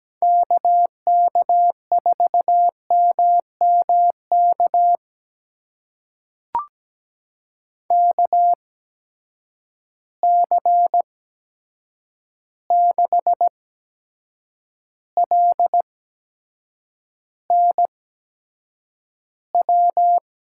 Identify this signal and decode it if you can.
Morse code, heard as a keyed tone, about 17 wpm, spaced as for 7 wpm, 700 Hz. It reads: KK4MMK KC6LNW
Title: USA Call Signs - 6 Characters - 8x Character Spacing - 17wpm